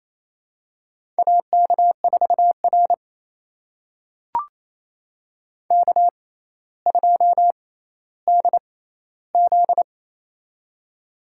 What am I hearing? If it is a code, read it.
Morse code, heard as a keyed tone, about 28 wpm, spaced as for 14 wpm, 700 Hz. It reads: AK4R K2DZ
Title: USA Call Signs - 4 Characters - 6x Character Spacing - 28wpm